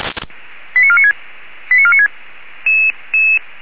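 Five-tone selective call (ZVEI), heard as quick runs of five tones, tones of about 80 ms. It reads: 89496 89496
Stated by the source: Five-tone ZVEI selective call received by SDR.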